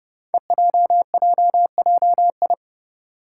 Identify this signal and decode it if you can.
Morse code, heard as a keyed tone, about 29 wpm, 700 Hz. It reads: EJJJI
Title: Five Challenging Letter-Number Combinations 30wpm